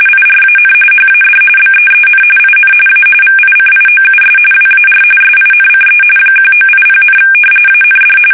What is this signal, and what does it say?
RTTY, about 45.45 baud. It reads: IRAN HAD BEEN IN A VICIOUS WAR SINCE 1980, AND
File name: fsk850 / battleshipnj